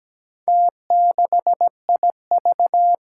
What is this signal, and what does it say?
Morse code, heard as a keyed tone, about 17 wpm, 700 Hz. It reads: T6IV
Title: Four Challenging Letter-Number Combinations 17wpm